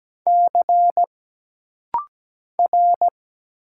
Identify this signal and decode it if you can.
Morse code, heard as a keyed tone, about 17 wpm, 700 Hz. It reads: C R
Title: Single Letter-Number Alt - Rapid-Fire 17wpm